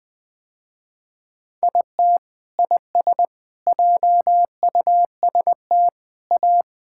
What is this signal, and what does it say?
Morse code, heard as a keyed tone, about 20 wpm, 700 Hz. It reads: IT IS JUST A
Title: Sentences with 8-Word Phrases - Spoken Beginning - 20wpm